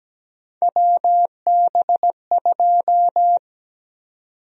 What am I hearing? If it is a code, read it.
Morse code, heard as a keyed tone, about 17 wpm, 700 Hz. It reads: WB2
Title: USA Call Sign Prefixes - 17wpm